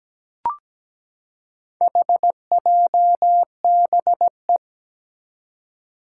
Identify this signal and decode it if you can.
Morse code, heard as a keyed tone, about 17 wpm, 700 Hz. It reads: HJBE
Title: Four Challenging Letter-Number Combinations 17wpm